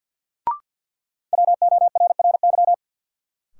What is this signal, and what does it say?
Morse code, heard as a keyed tone, about 53 wpm, 700 Hz. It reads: WORRY
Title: Sets of 5 Words - New Word Drill - 50wpm